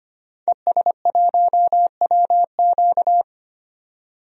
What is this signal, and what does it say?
Morse code, heard as a keyed tone, about 25 wpm, 700 Hz. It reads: ES1WQ
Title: DXCC-100 Call Signs - 5 Characters - 25wpm